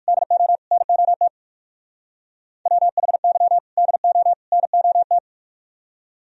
Morse code, heard as a keyed tone, about 45 wpm, 700 Hz. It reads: DO NOT WHY DO NOT
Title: Speed-Racing - Sets of 3 Words 45wpm